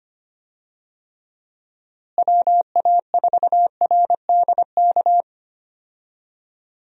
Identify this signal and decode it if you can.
Morse code, heard as a keyed tone, about 25 wpm, 700 Hz. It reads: WA4RDK
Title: USA Call Signs - 6 Characters - Repeated 3x - 25wpm